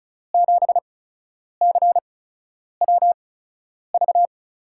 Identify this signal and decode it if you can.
Morse code, heard as a keyed tone, about 36 wpm, spaced as for 10 wpm, 700 Hz. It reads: 7CWV
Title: USA Call Signs - 6 Characters - 8x Character Spacing - 35wpm